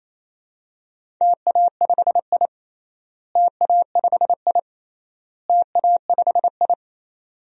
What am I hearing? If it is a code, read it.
Morse code, heard as a keyed tone, about 27 wpm, 700 Hz. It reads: TA5I TA5I TA5I
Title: DXCC-100 Call Signs - 4 Characters - Repeated 3x - 28wpm